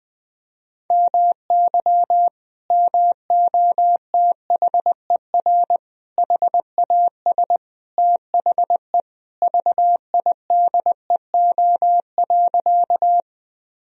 Morse code, heard as a keyed tone, about 21 wpm, 700 Hz. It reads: MY MOTHER HAS THE VIDEO.
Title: Sentences from Top 1000 Words - Review Part 4 - 20wpm